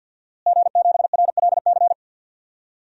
Morse code, heard as a keyed tone, about 52 wpm, 700 Hz. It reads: G7RLC
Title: DXCC-100 Call Signs - 5 Characters - Repeated 2x - 50wpm